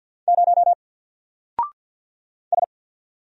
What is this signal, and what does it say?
Morse code, heard as a keyed tone, about 50 wpm, 700 Hz. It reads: 0 S
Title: Single Character (Letter:Number:Punctuation:Prosign) 50wpm